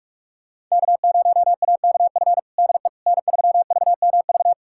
Morse code, heard as a keyed tone, about 41 wpm, 700 Hz. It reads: K0AKF DE N3VMV
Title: USA Callsign DE Callsign - 5 Characters - 45wpm